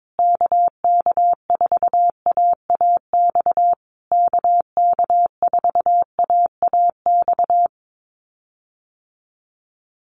Morse code, heard as a keyed tone, about 22 wpm, 700 Hz. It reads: KK4AAX KK4AAX
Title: USA Call Signs Repeated 3x 22wpm